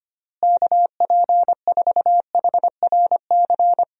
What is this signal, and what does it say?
Morse code, heard as a keyed tone, about 25 wpm, 700 Hz. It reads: KP4HRC
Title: USA Call Signs - 6 Characters - Repeated 2x - 25wpm